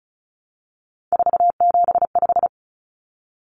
Morse code, heard as a keyed tone, about 34 wpm, 700 Hz. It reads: 475
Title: Common Cut Numbers - 3-Digit 35wpm